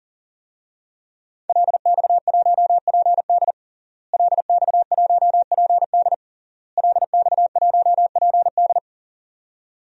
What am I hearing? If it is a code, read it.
Morse code, heard as a keyed tone, about 37 wpm, 700 Hz. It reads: LX1PD LX1PD LX1PD
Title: DXCC-100 Call Signs - 5 Characters - Repeated 3x - 40wpm